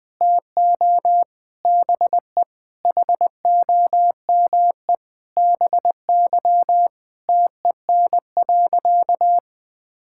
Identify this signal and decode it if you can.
Morse code, heard as a keyed tone, about 20 wpm, 700 Hz. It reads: TO BE HOME BY TEN.
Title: Sentences with 4-Word Phrases - 20wpm